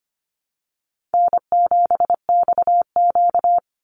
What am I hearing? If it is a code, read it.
Morse code, heard as a keyed tone, about 25 wpm, 700 Hz. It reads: N7XQ
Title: USA Call Signs 25wpm